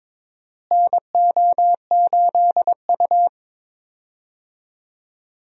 Morse code, heard as a keyed tone, about 23 wpm, 700 Hz. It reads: NO8U
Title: USA Call Signs - 4 Characters - Repeated 3x - 22wpm